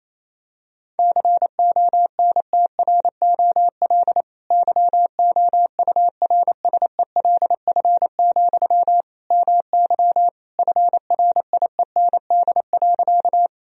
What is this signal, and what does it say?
Morse code, heard as a keyed tone, about 28 wpm, 700 Hz. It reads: CONTROL YOURSELF, MY FRIEND.
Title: Sentences from Top 1000 Words - Review Part 3 - 28wpm